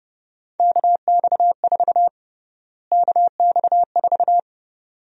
Morse code, heard as a keyed tone, about 31 wpm, 700 Hz. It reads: KX4 KX4
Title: USA Call Sign Prefixes - Repeated 2x - 30wpm